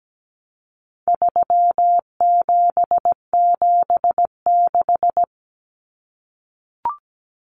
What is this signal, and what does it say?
Morse code, heard as a keyed tone, about 17 wpm, 700 Hz. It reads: 3776
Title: Numbers - 4-Digit 17wpm